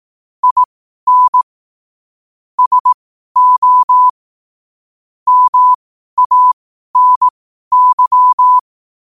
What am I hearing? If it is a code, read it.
Morse code, heard as a keyed tone, about 18 wpm, spaced as for 12 wpm, 1000 Hz. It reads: IN SO MANY
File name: qotdfiles / 12wpm